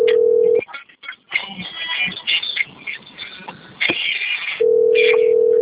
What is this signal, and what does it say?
Ringback tone (North America), 440 + 480 Hz.
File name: out- - - - - .12880